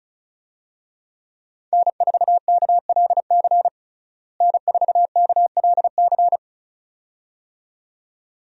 Morse code, heard as a keyed tone, about 34 wpm, 700 Hz. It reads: N4KLC N4KLC
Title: USA Call Signs - 5 Characters - Repeated 2x - 35wpm